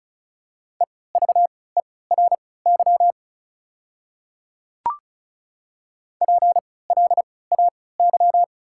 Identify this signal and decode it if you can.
Morse code, heard as a keyed tone, about 36 wpm, spaced as for 20 wpm, 700 Hz. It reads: EVERY PLAY
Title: 3x Character Spacing - Top 200 Words - 35wpm